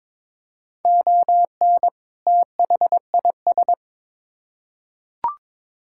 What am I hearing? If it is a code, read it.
Morse code, heard as a keyed tone, about 22 wpm, 700 Hz. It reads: ON THIS
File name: Build A Phrase - 2-Word Phrases - 22wpm